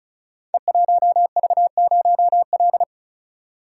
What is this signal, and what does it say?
Morse code, heard as a keyed tone, about 36 wpm, 700 Hz. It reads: E1V0L
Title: Code Groups - 5 Letter:Numbers - 35wpm